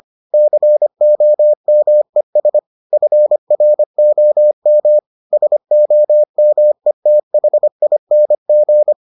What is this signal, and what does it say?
Morse code, heard as a keyed tone, about 25 wpm, 600 Hz. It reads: COMES FROM SOMETHING